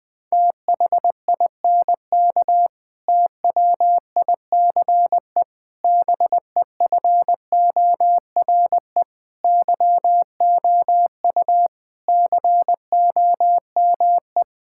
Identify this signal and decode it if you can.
Morse code, heard as a keyed tone, about 20 wpm, 700 Hz. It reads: THINK TWICE BEFORE YOU COME
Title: Sentences from Top 1600 Words 20wpm